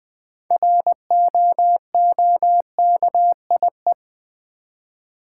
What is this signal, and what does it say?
Morse code, heard as a keyed tone, about 20 wpm, 700 Hz. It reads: ROOKIE
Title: Build A Word - 3 Letters At A Time - 6-Letter Words - 20wpm